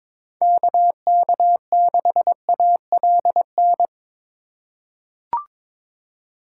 Morse code, heard as a keyed tone, about 22 wpm, 700 Hz. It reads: KK6ALN
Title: USA Call Signs - 6 Characters - 3x Character Spacing - 22wpm